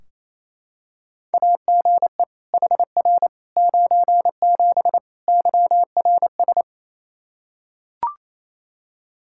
Morse code, heard as a keyed tone, about 28 wpm, 700 Hz. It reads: AGE HR 97 YRS
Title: Practice QSO - Exchange 3 - More Info - 28wpm